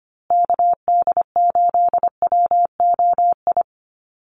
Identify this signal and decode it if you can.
Morse code, heard as a keyed tone, about 25 wpm, 700 Hz. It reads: KD8WOI
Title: USA Call Signs Repeated 2x 25wpm